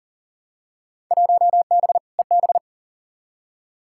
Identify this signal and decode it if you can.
Morse code, heard as a keyed tone, about 40 wpm, 700 Hz. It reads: 1B EB
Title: Field Day Exchanges - 40wpm